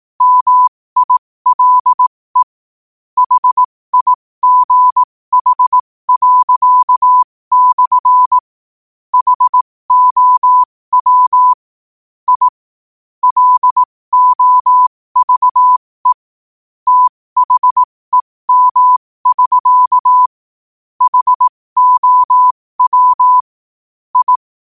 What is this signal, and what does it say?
Morse code, heard as a keyed tone, about 18 wpm, spaced as for 15 wpm, 1000 Hz. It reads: MILE HIGH./ HOW I LOVE THEM<SK> HOW I
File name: qotdfiles / 15wpm